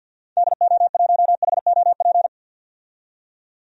Morse code, heard as a keyed tone, about 53 wpm, 700 Hz. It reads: DO1HOP
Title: DXCC-100 Call Signs - 6 Characters - 50wpm